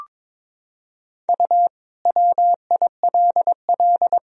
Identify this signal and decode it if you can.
Morse code, heard as a keyed tone, about 22 wpm, 700 Hz. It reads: U WILL
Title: Sentences from Top 200 Words and Abbreviations 22wpm